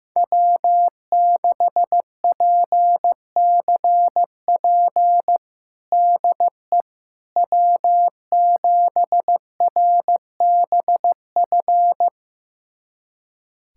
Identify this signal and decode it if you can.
Morse code, heard as a keyed tone, about 15 wpm, 700 Hz. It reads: W6PCP DE W7RBF
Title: USA Callsign DE Callsign - 5 Characters - 15wpm